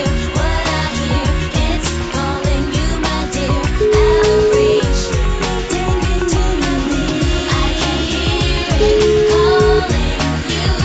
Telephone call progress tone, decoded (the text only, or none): ringback tone